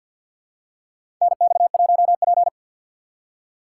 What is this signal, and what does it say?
Morse code, heard as a keyed tone, about 51 wpm, 700 Hz. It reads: NX1P
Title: USA Call Signs - 4 Characters - 3x Character Spacing - 50wpm